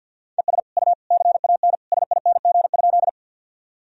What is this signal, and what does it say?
Morse code, heard as a keyed tone, about 52 wpm, 700 Hz. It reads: ES U CAN SING?
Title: Sentences from Top 400 Words and Abbreviations 50wpm